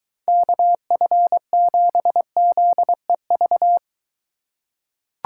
Morse code, heard as a keyed tone, about 23 wpm, 700 Hz. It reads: KF7ZEV